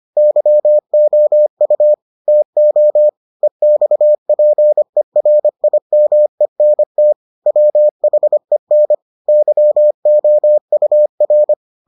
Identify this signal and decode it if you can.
Morse code, heard as a keyed tone, about 25 wpm, 600 Hz. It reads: YOU TO EXPERIMENT WHEN YOUR